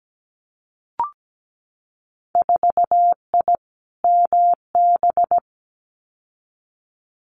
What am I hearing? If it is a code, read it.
Morse code, heard as a keyed tone, about 17 wpm, 700 Hz. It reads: 4I MB
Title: Winter Field Day Exchanges Encore 17wpm